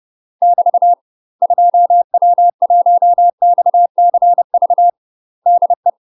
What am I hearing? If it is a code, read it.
Morse code, heard as a keyed tone, about 30 wpm, 700 Hz. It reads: <BT> 2W1XCV DE
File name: UK MASTER QSOs TRAINING / 30WPM / UK MASTER QSOs 0039